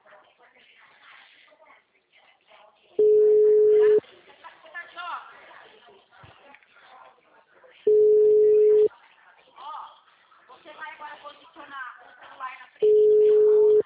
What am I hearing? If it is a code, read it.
Ringback tone (Europe, 425 Hz), with speech.
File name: out- -2016- - - .272